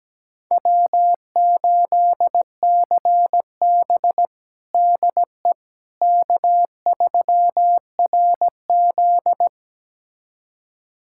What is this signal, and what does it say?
Morse code, heard as a keyed tone, about 17 wpm, 700 Hz. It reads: W8CB DE K3RZ
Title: USA Callsign DE Callsign - 4 Characters - 17wpm